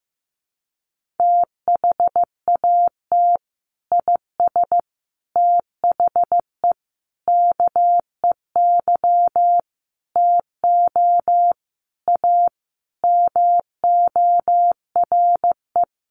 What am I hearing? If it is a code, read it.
Morse code, heard as a keyed tone, about 14 wpm, 700 Hz. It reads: THAT IS THE KEY TO A MORE
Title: Sentences from Top 1400 Words 15wpm